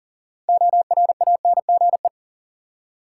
Morse code, heard as a keyed tone, about 38 wpm, 700 Hz. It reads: ORANGE
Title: Six Letter Words - 40wpm